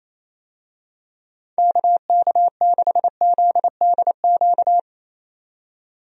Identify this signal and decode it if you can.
Morse code, heard as a keyed tone, about 27 wpm, 700 Hz. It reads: KK6ZDQ
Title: USA Call Signs - 6 Characters - 28wpm